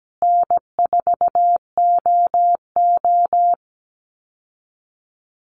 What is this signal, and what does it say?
Morse code, heard as a keyed tone, about 17 wpm, 700 Hz. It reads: N4OO
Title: Easy Call Signs Repeated 3x - 17wpm